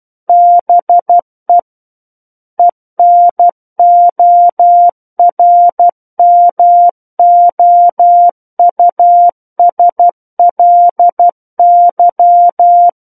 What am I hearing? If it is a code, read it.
Morse code, heard as a keyed tone, about 12 wpm, 700 Hz. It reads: BE ENORMOUSLY
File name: Book One / 03-Chapter 3 - 12 WPM